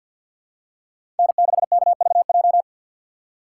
Morse code, heard as a keyed tone, about 50 wpm, 700 Hz. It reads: N6KVJ
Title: USA Call Signs - 5 Characters - 2x Character Spacing - 50wpm